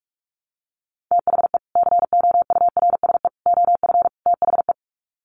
Morse code, heard as a keyed tone, about 44 wpm, 700 Hz. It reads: THE COURSE OF THE
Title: Sets of 4 Words 45wpm